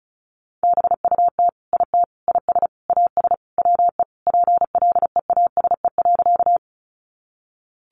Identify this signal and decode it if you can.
Morse code, heard as a keyed tone, about 34 wpm, 700 Hz. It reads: BUT IT IS AS WE PLEASE.
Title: Sentences from Top 1500 Words 35wpm